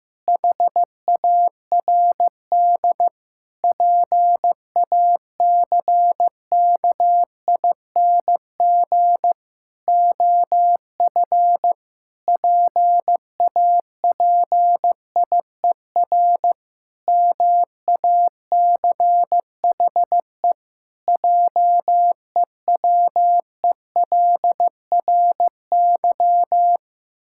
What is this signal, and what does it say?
Morse code, heard as a keyed tone, about 16 wpm, 700 Hz. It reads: HARD PACKING OF PAPIER MACHE JEWELRY
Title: Pangram Challenge v2 - 15wpm